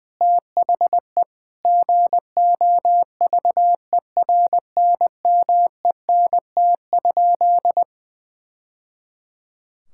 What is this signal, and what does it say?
Morse code, heard as a keyed tone, about 21 wpm, 700 Hz. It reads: THE GOVERNMENT?
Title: Sentences from Top 1000 Words - Review Part 5 - 20wpm